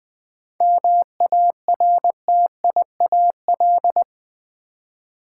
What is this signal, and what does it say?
Morse code, heard as a keyed tone, about 20 wpm, 700 Hz. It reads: MARTIAL
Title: 7 Letter Words - First 3 Letters Given - 20wpm